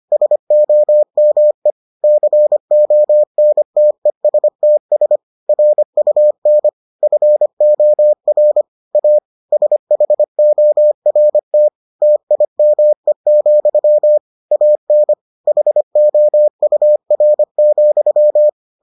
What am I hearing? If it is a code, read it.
Morse code, heard as a keyed tone, about 25 wpm, 600 Hz. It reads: SOME CONTESTS RUN FOR A SHORT TIME, AN HOUR,